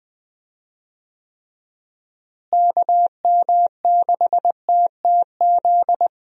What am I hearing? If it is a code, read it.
Morse code, heard as a keyed tone, about 20 wpm, 700 Hz. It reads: KM6TTZ
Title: USA Call Signs - 6 Characters - Repeated 2x - 20wpm